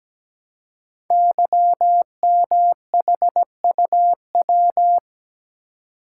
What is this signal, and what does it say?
Morse code, heard as a keyed tone, about 17 wpm, 700 Hz. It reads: YMHUW
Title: Code Groups - 5 Letter:Numbers - 17wpm